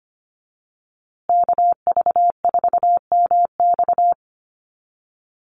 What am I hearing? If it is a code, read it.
Morse code, heard as a keyed tone, about 25 wpm, 700 Hz. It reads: KV4MX
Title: USA Call Signs Repeated 3x 25wpm